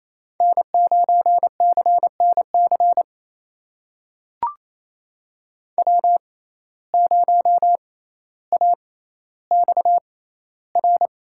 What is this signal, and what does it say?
Morse code, heard as a keyed tone, about 28 wpm, spaced as for 14 wpm, 700 Hz. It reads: N9CNC W0AXR
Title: USA Call Signs - 5 Characters - 6x Character Spacing - 28wpm